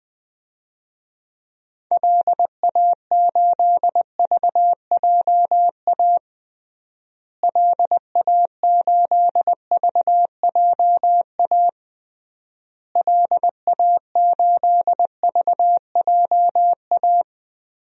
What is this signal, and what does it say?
Morse code, heard as a keyed tone, about 20 wpm, 700 Hz. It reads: LA8VJA LA8VJA LA8VJA
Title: DXCC-100 Call Signs - 6 Characters - Repeated 3x - 20wpm